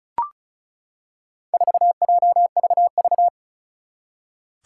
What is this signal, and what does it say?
Morse code, heard as a keyed tone, about 36 wpm, 700 Hz. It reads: 4JVV
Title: Four Challenging Letter-Number Combinations 35wpm